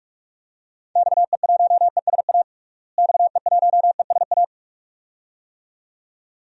Morse code, heard as a keyed tone, about 44 wpm, 700 Hz. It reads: XE1ESA XE1ESA
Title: DXCC-100 Call Signs - 6 Characters - Repeated 2x - 45wpm